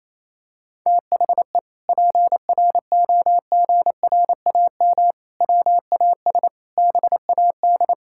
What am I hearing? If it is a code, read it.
Morse code, heard as a keyed tone, about 28 wpm, 700 Hz. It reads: THE PROGRAM WAS BAD
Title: Sentences from Top 1000 Words - Review Part 5 - 28wpm